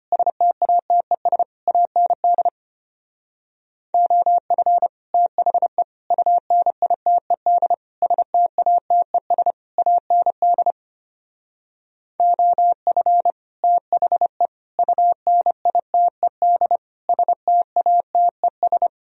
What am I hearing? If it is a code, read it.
Morse code, heard as a keyed tone, about 30 wpm, 700 Hz. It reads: STATES AND OF THE UNITED STATES AND OF THE UNITED STATES
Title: Speed-Racing - Sets of 5 Words 25wpm